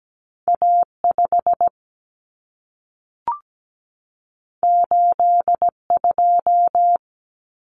Morse code, heard as a keyed tone, about 17 wpm, 700 Hz. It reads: A5 82
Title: Common Cut Numbers - 2-Digit 17wpm